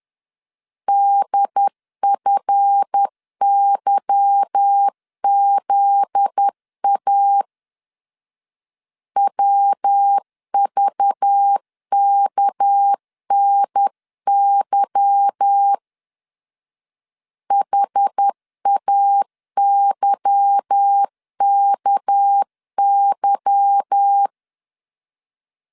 Morse code, heard as a keyed tone, about 10 wpm, 800 Hz. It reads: DFYZA WVKNY HAYKY
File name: LEZIONI / 8 YJ45 / 8 Y 10wpm 2615